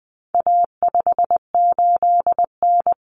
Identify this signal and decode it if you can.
Morse code, heard as a keyed tone, about 20 wpm, 700 Hz. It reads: A58N
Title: Common Cut Numbers - 4-Digit 20wpm